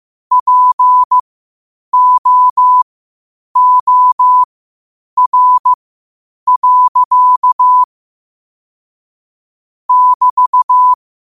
Morse code, heard as a keyed tone, about 15 wpm, spaced as for 8 wpm, 1000 Hz. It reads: POOR. <BT>
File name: qotdfiles / 8wpm